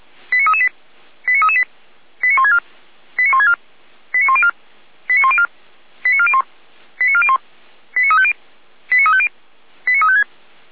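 Five-tone selective call (ZVEI), heard as quick runs of five tones, tones of about 70 ms. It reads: 89338 89338 89164 89164 89104 89104 89401 89401 89350 89350 89356
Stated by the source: Five-tone ZVEI selective call 89701 received on an SDR.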